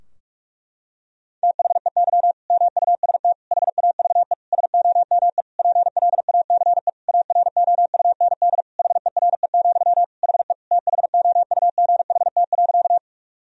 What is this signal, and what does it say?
Morse code, heard as a keyed tone, about 46 wpm, 700 Hz. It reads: THEY MUST HAVE SOME PLACE AROUND HERE, HE THOUGHT.
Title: Sentences from Top 1000 Words - Review Part 3 - 45wpm